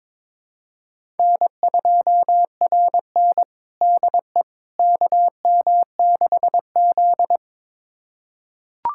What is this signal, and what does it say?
Morse code, heard as a keyed tone, about 22 wpm, 700 Hz. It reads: N2RN DE KM6Z
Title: USA Callsign DE Callsign - 4 Characters - 22wpm